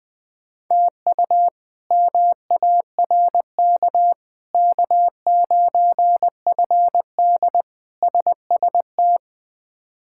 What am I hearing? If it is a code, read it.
Morse code, heard as a keyed tone, about 20 wpm, 700 Hz. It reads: TU MARK K9FD SST
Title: K1USN SST QSO Practice - 20wpm